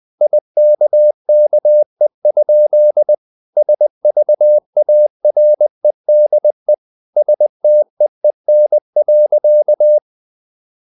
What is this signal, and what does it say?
Morse code, heard as a keyed tone, about 20 wpm, 600 Hz. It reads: IKKE? SVAREDE STEEN.
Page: Ferie 20 wpm